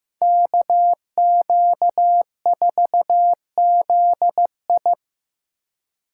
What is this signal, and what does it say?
Morse code, heard as a keyed tone, about 18 wpm, 700 Hz. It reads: KQ4ZI
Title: Speed Racing - USA 2x2 Call Signs 15wpm